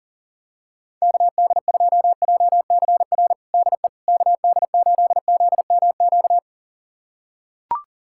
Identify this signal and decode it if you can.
Morse code, heard as a keyed tone, about 37 wpm, 700 Hz. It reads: KD2JCR DE KD8ZMQ
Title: USA Callsign DE Callsign 40wpm